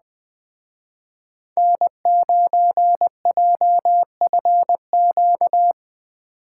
Morse code, heard as a keyed tone, about 19 wpm, 700 Hz. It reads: N9JFQ
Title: Speed Racing - USA 1x3 Call Signs 17wpm